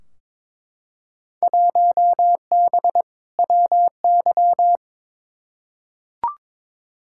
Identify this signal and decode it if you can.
Morse code, heard as a keyed tone, about 22 wpm, 700 Hz. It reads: 1B WY
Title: Field Day Exchanges Encore - 22wpm